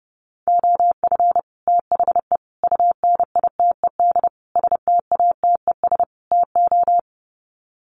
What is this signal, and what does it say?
Morse code, heard as a keyed tone, about 30 wpm, 700 Hz. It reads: OF THE UNITED STATES TO
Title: Sets of 5 Words Encore 30wpm